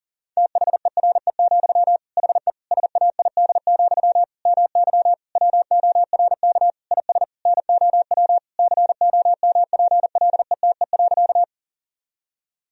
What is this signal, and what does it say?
Morse code, heard as a keyed tone, about 40 wpm, 700 Hz. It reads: THERE, HE SAID, MY WORK IS NOW COMPLETE.
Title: Sentences from Top 1000 Words - Review Part 2 - 40wpm